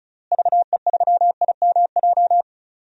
Morse code, heard as a keyed tone, about 34 wpm, 700 Hz. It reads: VE3IMJ
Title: DXCC-100 Call Signs - 6 Characters - Repeated 2x - 35wpm